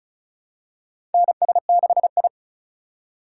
Morse code, heard as a keyed tone, about 36 wpm, 700 Hz. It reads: NS6I
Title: USA Call Signs - 4 Characters - 4x Character Spacing - 35wpm